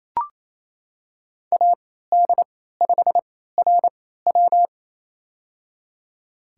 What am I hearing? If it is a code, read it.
Morse code, heard as a keyed tone, about 27 wpm, spaced as for 16 wpm, 700 Hz. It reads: AD5RW
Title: USA Call Signs - 3x Character Spacing - 28wpm